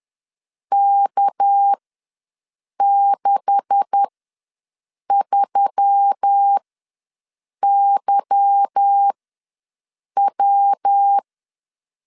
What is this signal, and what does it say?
Morse code, heard as a keyed tone, about 10 wpm, spaced as for 6 wpm, 800 Hz. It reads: K63YW
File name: LEZIONI / 10 SIGNS / 10A SLASH ws 10wpm 4121